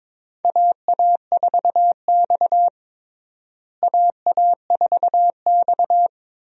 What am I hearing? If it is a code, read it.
Morse code, heard as a keyed tone, about 22 wpm, 700 Hz. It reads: AA4X AA4X
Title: USA Call Signs - 4 Characters - Repeated 3x - 22wpm